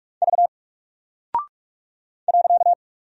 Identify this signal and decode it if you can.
Morse code, heard as a keyed tone, about 44 wpm, 700 Hz. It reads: V .
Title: Single Character (Letter:Number:Punctuation:Prosign) 45wpm